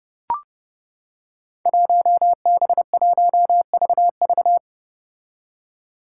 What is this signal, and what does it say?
Morse code, heard as a keyed tone, about 30 wpm, 700 Hz. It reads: 1B1VV
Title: Five Challenging Letter-Number Combinations 30wpm